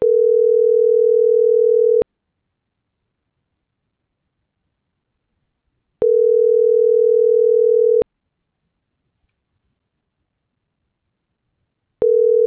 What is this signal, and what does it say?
Ringback tone (North America), 440 + 480 Hz.